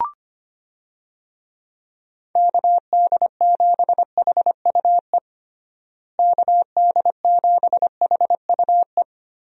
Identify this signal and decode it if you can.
Morse code, heard as a keyed tone, about 25 wpm, 700 Hz. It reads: KD7HUE KD7HUE
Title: USA Call Signs - 6 Characters - Repeated 3x - 25wpm